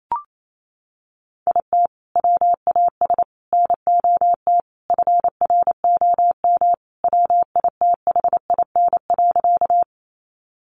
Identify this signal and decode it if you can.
Morse code, heard as a keyed tone, about 28 wpm, 700 Hz. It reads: IT WAS NOT FROM WITHIN.
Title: Sentences from Top 700 Words 28wpm